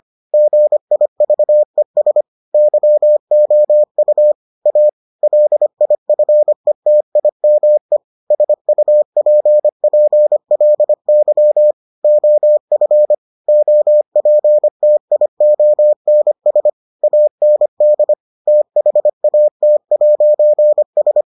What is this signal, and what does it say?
Morse code, heard as a keyed tone, about 25 wpm, 600 Hz. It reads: GIVES YOU A LIFETIME SUPPLY OF OPTIONS AND THAT'S